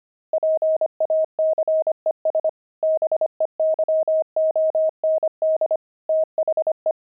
Morse code, heard as a keyed tone, about 25 wpm, 630 Hz. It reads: PACES BEYOND THE
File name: pangram-fs-25-cs-25-javelins